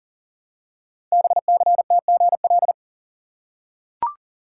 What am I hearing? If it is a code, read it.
Morse code, heard as a keyed tone, about 40 wpm, 700 Hz. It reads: BCTGL
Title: Code Groups - 5 Letters - 40wpm